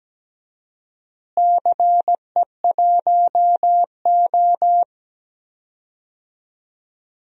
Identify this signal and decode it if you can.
Morse code, heard as a keyed tone, about 17 wpm, 700 Hz. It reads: CE1O
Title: DXCC-100 Call Signs - 4 Characters - Repeated 3x - 17wpm